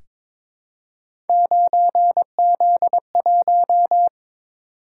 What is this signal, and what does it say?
Morse code, heard as a keyed tone, about 22 wpm, 700 Hz. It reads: 9Z1
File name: practice / 22wpm / Three Letter-Number Combinations - 22wpm